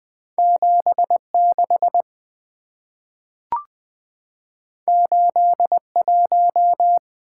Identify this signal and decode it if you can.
Morse code, heard as a keyed tone, about 20 wpm, 700 Hz. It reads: 76 81
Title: Two Letter-Number Combinations - 20wpm